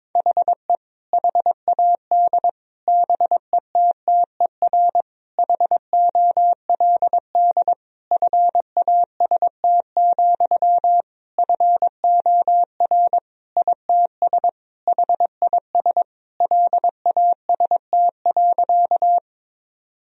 Morse code, heard as a keyed tone, about 22 wpm, 700 Hz. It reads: HE HAD BETTER HOLD FAST, FOR ITS HIS LAST.
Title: Sentences from Top 1000 Words - Review Part 2 - 22wpm